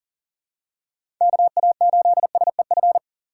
Morse code, heard as a keyed tone, about 40 wpm, 700 Hz. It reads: KA8SEF
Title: USA Call Signs - 6 Characters - 6x Character Spacing - 40wpm